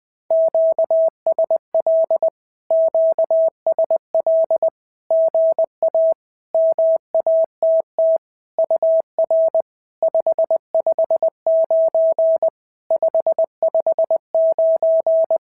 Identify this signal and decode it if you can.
Morse code, heard as a keyed tone, about 20 wpm, 650 Hz. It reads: QSL QSL GA MATT UR 559 559
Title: SOTA QSO Practice 20wpm